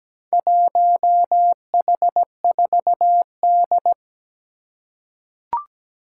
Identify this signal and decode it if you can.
Morse code, heard as a keyed tone, about 17 wpm, 700 Hz. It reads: 1H4D
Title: Four Challenging Letter-Number Combinations 17wpm